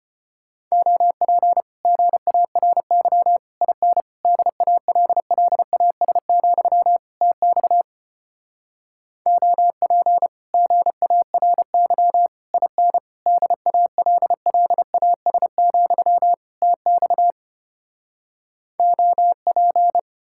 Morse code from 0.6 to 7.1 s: OP GARY IN DALLAS,

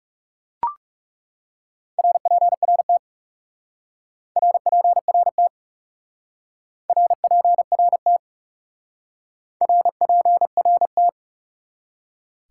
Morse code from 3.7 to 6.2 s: RPRT

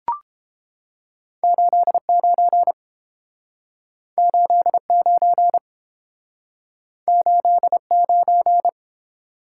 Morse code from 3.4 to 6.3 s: 89

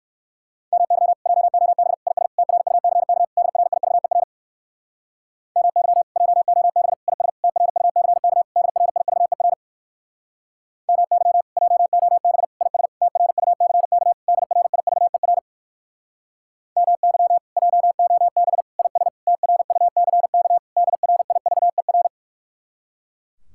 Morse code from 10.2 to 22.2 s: MY JOB IS TRUCK DRIVER MY JOB IS TRUCK DRIVER

